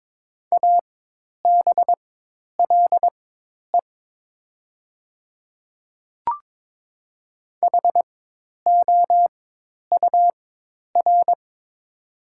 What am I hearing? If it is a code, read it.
Morse code, heard as a keyed tone, about 22 wpm, spaced as for 10 wpm, 700 Hz. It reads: ABLE HOUR
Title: 4x Character Spacing - Top 500 Words - 22wpm